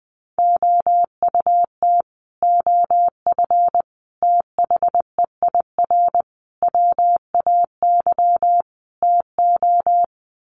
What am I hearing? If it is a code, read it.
Morse code, heard as a keyed tone, about 20 wpm, 700 Hz. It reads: OUT OF THEIR WAY TO
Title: Sets of 5 Words Encore2 20wpm